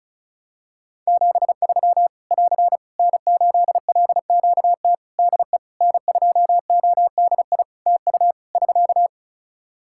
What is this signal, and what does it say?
Morse code, heard as a keyed tone, about 36 wpm, 700 Hz. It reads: 73 <AR> N8LQT DE N2ODI TU <SK>
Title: K1USN SST QSO Practice 35wpm